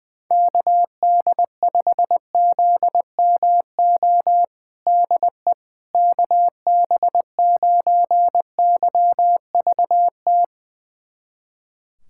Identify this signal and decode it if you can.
Morse code, heard as a keyed tone, about 20 wpm, 700 Hz. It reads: KD5ZMO DE KB9YVT
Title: USA Callsign DE Callsign - 6 Characters - 20wpm